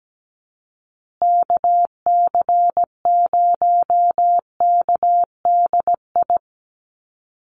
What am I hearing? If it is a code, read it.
Morse code, heard as a keyed tone, about 17 wpm, 700 Hz. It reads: KC0KDI
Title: USA Call Signs Repeated 3x 17wpm